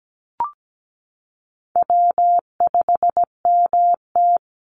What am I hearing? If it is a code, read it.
Morse code, heard as a keyed tone, about 17 wpm, 700 Hz. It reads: W5MT
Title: Easy Call Signs Repeated 3x - 17wpm